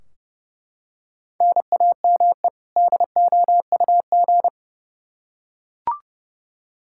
Morse code, heard as a keyed tone, about 32 wpm, 700 Hz. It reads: NAME DOUG
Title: Practice QSO - Exchange 1 - Beginning - 30wpm